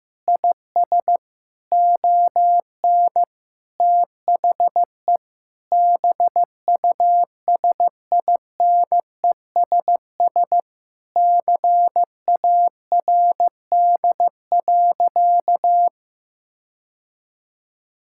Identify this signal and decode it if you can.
Morse code, heard as a keyed tone, about 15 wpm, 700 Hz. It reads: IS ON THE BUSINESS CARD.
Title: Sentences from Top 1000 Words - Review Part 6 - 15wpm